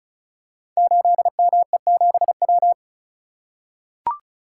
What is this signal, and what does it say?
Morse code, heard as a keyed tone, about 36 wpm, 700 Hz. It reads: 8ME7W
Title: Code Groups - 5 Letter:Numbers - 35wpm